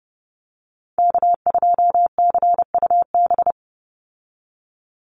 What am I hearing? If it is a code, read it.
Morse code, heard as a keyed tone, about 30 wpm, 700 Hz. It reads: K2CUB
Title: Call Signs - 30wpm